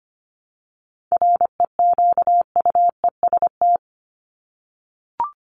Morse code, heard as a keyed tone, about 25 wpm, 700 Hz. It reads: REQUEST
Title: Top 1751-2000 Words 25wpm